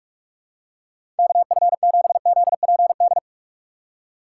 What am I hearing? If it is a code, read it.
Morse code, heard as a keyed tone, about 45 wpm, 700 Hz. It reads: KF7ZPD
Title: USA Call Signs - 3x Character Spacing - 45wpm